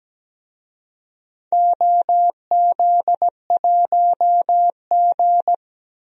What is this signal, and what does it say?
Morse code, heard as a keyed tone, about 17 wpm, 700 Hz. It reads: OZ1G
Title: DXCC-100 Call Signs - 4 Characters - 17wpm